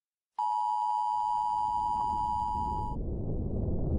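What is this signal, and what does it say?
Emergency Alert System (SAME) tones (the 853 and 960 Hz attention tone). It is heard from 0.3 to 3.0 s.
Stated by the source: Emergency broadcast and ham radio signals in a fictional audio drama after a world-ending eruption.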